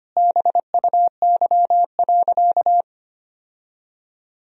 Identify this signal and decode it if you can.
Morse code, heard as a keyed tone, about 25 wpm, 700 Hz. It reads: BUY.
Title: Sentences from Top 1000 Words - Review Part 4 - 25wpm